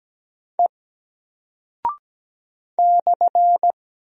Morse code, heard as a keyed tone, about 17 wpm, 700 Hz. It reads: E /
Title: Single Character (Letter:Number:Punctuation:Prosign) Alt 17wpm